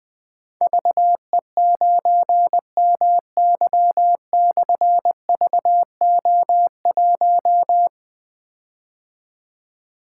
Morse code, heard as a keyed tone, about 20 wpm, 700 Hz. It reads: VE9MY/VO1
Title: Difficult Call Signs 20wpm